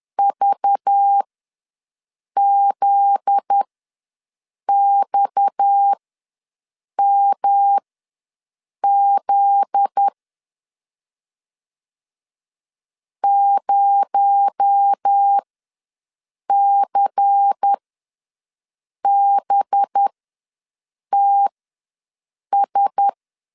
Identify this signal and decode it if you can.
Morse code, heard as a keyed tone, about 10 wpm, spaced as for 6 wpm, 800 Hz. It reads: VZXMZ 0CBTS
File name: LEZIONI / 7 GF9Z / 7 GF9Z ws 10wpm 3919